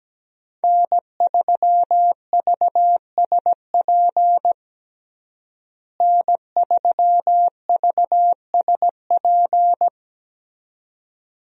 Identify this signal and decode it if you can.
Morse code, heard as a keyed tone, about 17 wpm, 700 Hz. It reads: N3VSP N3VSP
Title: USA Call Signs - 5 Characters - Repeated 2x - 17wpm